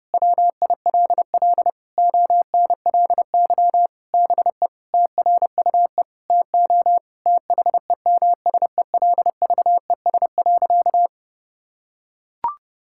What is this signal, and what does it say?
Morse code, heard as a keyed tone, about 32 wpm, 700 Hz. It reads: WILL ONLY BE TRUE TO THEMSELVES.
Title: Sentences from Top 1000 Words - Review Part 6 - 30wpm